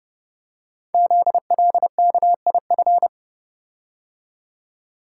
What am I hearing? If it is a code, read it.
Morse code, heard as a keyed tone, about 30 wpm, 700 Hz. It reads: ZLKIF
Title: Code Groups - 5 Letters - 30wpm